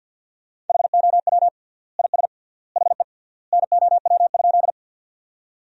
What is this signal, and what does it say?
Morse code, heard as a keyed tone, about 50 wpm, 700 Hz. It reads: HOW IS HE NOW?
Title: 3x Word-Spacing - Sentences from Top 100 Words - 50wpm